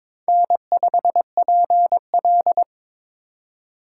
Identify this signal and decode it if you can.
Morse code, heard as a keyed tone, about 22 wpm, 700 Hz. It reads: N5PL
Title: USA Call Signs - 4 Characters - 4x Character Spacing - 22wpm